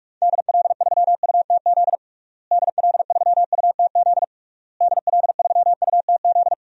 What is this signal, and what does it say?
Morse code, heard as a keyed tone, about 41 wpm, 700 Hz. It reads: DL3UTZ DL3UTZ DL3UTZ
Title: DXCC-100 Call Signs - 6 Characters - Repeated 3x - 45wpm